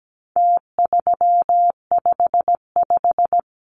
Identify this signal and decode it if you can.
Morse code, heard as a keyed tone, about 17 wpm, 700 Hz. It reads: T355
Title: Numbers - Contest Style - 4-Digit 17wpm